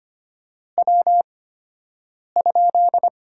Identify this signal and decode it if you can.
Morse code, heard as a keyed tone, about 25 wpm, spaced as for 7 wpm, 690 Hz. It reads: W?
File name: Top 90 QSO Elements - 8x Character Spacing - 25wpm